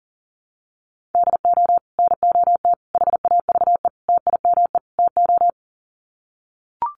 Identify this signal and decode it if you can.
Morse code, heard as a keyed tone, about 40 wpm, 700 Hz. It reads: DO NOT HAVE TIME TO
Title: Sets of 5 Words 40wpm